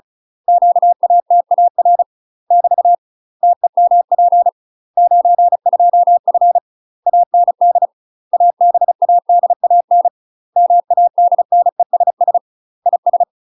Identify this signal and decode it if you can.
Morse code, heard as a keyed tone, about 35 wpm, 700 Hz. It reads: QATAR <BT> TEMP 92F AND ABADAN MADNESS IS
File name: UK MASTER QSOs TRAINING / 35WPM / UK MASTER QSOs 0020